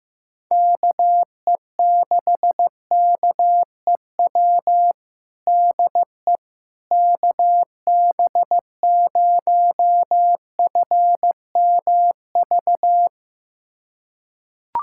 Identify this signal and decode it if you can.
Morse code, heard as a keyed tone, about 15 wpm, 700 Hz. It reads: KE6KEW DE KB0FMV
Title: USA Callsign DE Callsign 15wpm